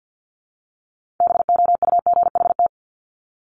Morse code, heard as a keyed tone, about 50 wpm, 700 Hz. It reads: BOUGHT
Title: Multi-letter Phonemes Repeated 3x 50wpm